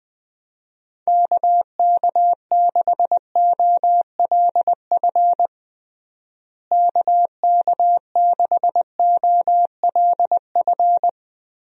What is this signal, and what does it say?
Morse code, heard as a keyed tone, about 21 wpm, 700 Hz. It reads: KK6OLF KK6OLF
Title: USA Call Signs - 6 Characters - Repeated 3x - 20wpm